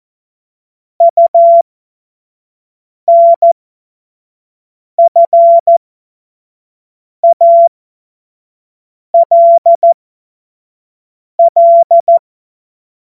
Morse code, heard as a keyed tone, about 14 wpm, spaced as for 5 wpm, 680 Hz. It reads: UNFALL
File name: Morsekurs / Witze / Witz 02 Januar 2024 25 F70